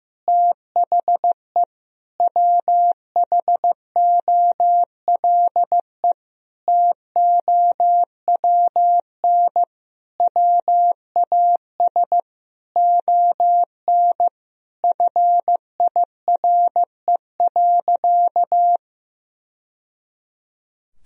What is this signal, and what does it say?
Morse code, heard as a keyed tone, about 15 wpm, 700 Hz. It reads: THE WHOLE TOWN WAS ON FIRE.
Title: Sentences from Top 1000 Words - Review Part 2 - 15wpm